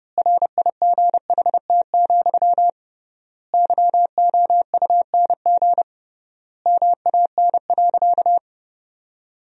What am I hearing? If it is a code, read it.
Morse code, heard as a keyed tone, about 30 wpm, 700 Hz. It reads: RIGHT, YOUNG MAN.
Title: 3x Word-Spacing - Sentences from Top 400 Words - 30wpm